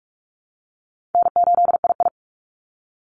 Morse code, heard as a keyed tone, about 44 wpm, 700 Hz. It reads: N8II
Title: Call Signs - 45wpm